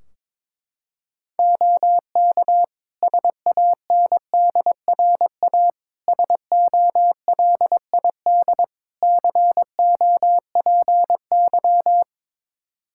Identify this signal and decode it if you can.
Morse code, heard as a keyed tone, about 22 wpm, 700 Hz. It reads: OK SANDRA SOLID COPY
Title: Practice QSO - Exchange 3 - More Info - 22wpm